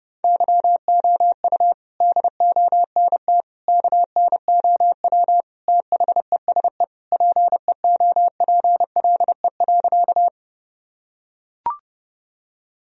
Morse code, heard as a keyed tone, about 32 wpm, 700 Hz. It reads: YOU DONT KNOW THESE PEOPLE.
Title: Sentences from Top 1000 Words - Review Part 1 - 30wpm